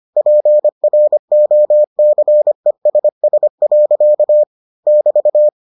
Morse code, heard as a keyed tone, about 25 wpm, 600 Hz. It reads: PROCESS. <BT>